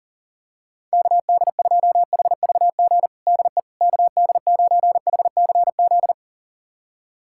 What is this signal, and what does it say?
Morse code, heard as a keyed tone, about 40 wpm, 700 Hz. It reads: KD2HVG DE KD9HCZ
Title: USA Callsign DE Callsign - 6 Characters - 40wpm